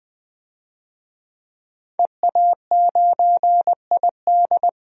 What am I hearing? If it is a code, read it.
Morse code, heard as a keyed tone, about 20 wpm, 700 Hz. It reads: EA9ID
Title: DXCC-100 Call Signs - 5 Characters - 20wpm